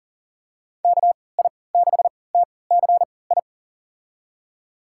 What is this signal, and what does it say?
Morse code, heard as a keyed tone, about 40 wpm, spaced as for 23 wpm, 700 Hz. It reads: KI6TCI
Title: USA Call Signs - 6 Characters - 3x Character Spacing - 40wpm